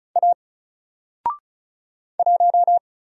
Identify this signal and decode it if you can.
Morse code, heard as a keyed tone, about 34 wpm, 700 Hz. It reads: A 1
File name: Single-Letter-Number-35wpm